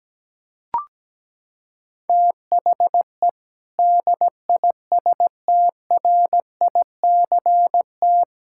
Morse code, heard as a keyed tone, about 17 wpm, 700 Hz. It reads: THE DISTRICT
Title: Sentences from Top 1600 Words 17wpm